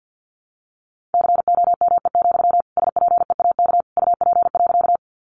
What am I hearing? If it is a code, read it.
Morse code, heard as a keyed tone, about 50 wpm, 700 Hz. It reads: COME, SPEAK UP.
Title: Sentences from Top 1100 Words 50wpm